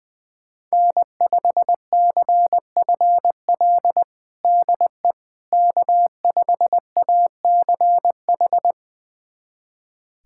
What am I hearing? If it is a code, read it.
Morse code, heard as a keyed tone, about 20 wpm, 700 Hz. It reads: N5CFL DE K5ACH
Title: USA Callsign DE Callsign - 5 Characters - 20wpm